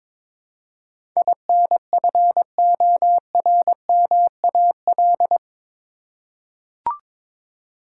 Morse code, heard as a keyed tone, about 22 wpm, 700 Hz. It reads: INFORMAL
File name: practice / 22wpm / Eight Letter Words - 22wpm